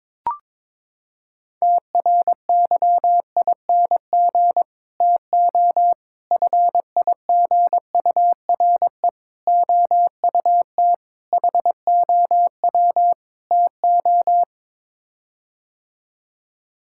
Morse code, heard as a keyed tone, about 22 wpm, 700 Hz. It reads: TRYING TO FIGURE OUT HOW TO
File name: practice / 22wpm / Sets of 6 Words - 22wpm